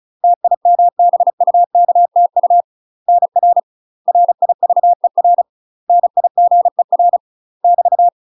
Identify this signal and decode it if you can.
Morse code, heard as a keyed tone, about 34 wpm, 700 Hz. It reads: TIMBUKTU NR RIVER NIGER <BT>
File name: UK MASTER QSOs TRAINING/35WPM